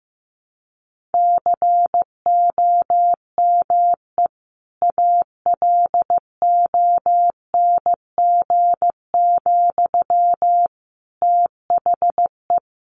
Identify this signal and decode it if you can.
Morse code, heard as a keyed tone, about 15 wpm, 700 Hz. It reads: COME ALONG, THE
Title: Sentences from Top 700 Words 15wpm